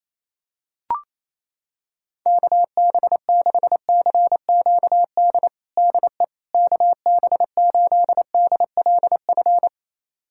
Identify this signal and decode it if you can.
Morse code, heard as a keyed tone, about 28 wpm, 700 Hz. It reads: KB6CQD DE KB8DLF
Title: USA Callsign DE Callsign 28wpm